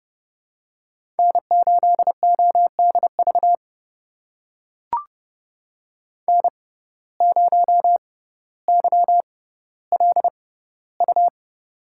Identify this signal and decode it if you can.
Morse code, heard as a keyed tone, about 30 wpm, spaced as for 15 wpm, 700 Hz. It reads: N8ODV N0YLU
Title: USA Call Signs - 5 Characters - 6x Character Spacing - 30wpm